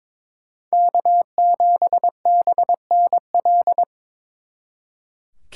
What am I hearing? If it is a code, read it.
Morse code, heard as a keyed tone, about 26 wpm, 700 Hz. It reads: K7BNL
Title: Speed Racing - USA 1x3 Call Signs 22wpm